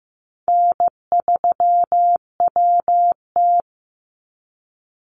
Morse code, heard as a keyed tone, about 15 wpm, 700 Hz. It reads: N3WT
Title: Easy Call Signs - 15wpm